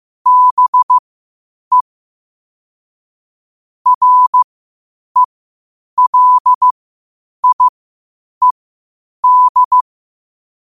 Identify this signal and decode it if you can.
Morse code, heard as a keyed tone, about 15 wpm, spaced as for 8 wpm, 1000 Hz. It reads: BE RELIED
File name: qotdfiles / 8wpm